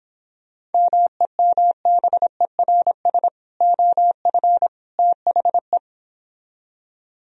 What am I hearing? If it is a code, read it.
Morse code, heard as a keyed tone, about 29 wpm, 700 Hz. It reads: MEMBERS OF THE